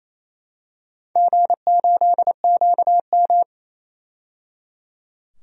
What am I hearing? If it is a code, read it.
Morse code, heard as a keyed tone, about 28 wpm, 700 Hz. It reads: G8QM
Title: DXCC-100 Call Signs - 4 Characters - 28wpm